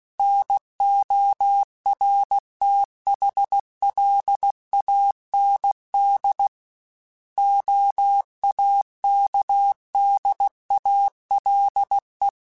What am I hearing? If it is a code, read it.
Morse code, heard as a keyed tone, about 16 wpm, 800 Hz. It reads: NORTHLAND OAKDALE
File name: fromSMs / FromSMs mp3s / Cities@15